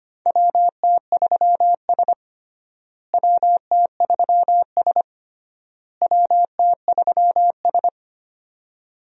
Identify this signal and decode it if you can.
Morse code, heard as a keyed tone, about 25 wpm, 700 Hz. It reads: WT3S WT3S WT3S
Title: USA Call Signs - 4 Characters - Repeated 3x - 25wpm